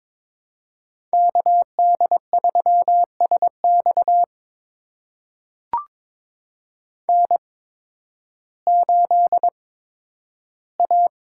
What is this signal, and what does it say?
Morse code, heard as a keyed tone, about 22 wpm, spaced as for 12 wpm, 700 Hz. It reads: KD3SX N8A
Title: USA Call Signs - 5 Characters - 8x Character Spacing - 22wpm